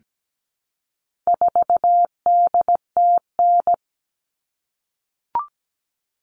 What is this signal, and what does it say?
Morse code, heard as a keyed tone, about 17 wpm, 700 Hz. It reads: 4DTN
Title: All Cut Numbers - 4-Digit 17wpm